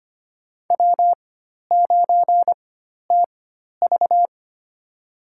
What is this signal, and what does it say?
Morse code, heard as a keyed tone, about 25 wpm, spaced as for 12 wpm, 700 Hz. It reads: W9TV
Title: USA Call Signs - 4 Characters - 4x Character Spacing - 25wpm